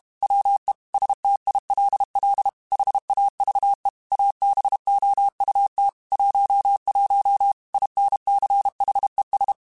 Morse code, heard as a keyed tone, about 32 wpm, 800 Hz. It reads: WE STILL HAVE ABOUT 11 INCHES
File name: QSOs / QSOs-mp3s / WA7QOX 1@30